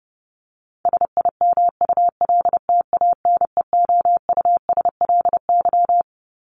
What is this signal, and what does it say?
Morse code, heard as a keyed tone, about 28 wpm, 700 Hz. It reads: SIMULTANEOUSLY
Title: LY Suffix - Top 100 Words with Base Words 30wpm